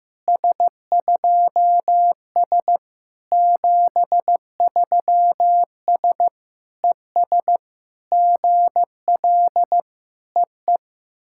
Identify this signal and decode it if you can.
Morse code, heard as a keyed tone, about 14 wpm, 700 Hz. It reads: S2S 73S ES GL EE
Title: SOTA QSO Practice 15wpm